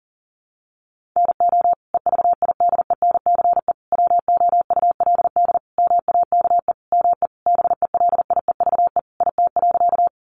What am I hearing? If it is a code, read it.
Morse code, heard as a keyed tone, about 40 wpm, 700 Hz. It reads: NO EVIDENCE WOULD MAKE ME BELIEVE IT.
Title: Sentences from Top 1300 Words 40wpm